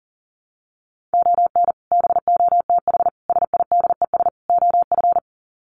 Morse code, heard as a keyed tone, about 40 wpm, 700 Hz. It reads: ON BOTH SIDES OF
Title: Sets of 4 Words 40wpm